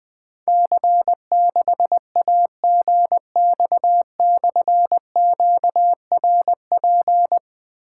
Morse code, heard as a keyed tone, about 20 wpm, 700 Hz. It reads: C6AGX/QRP
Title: Difficult Call Signs 20wpm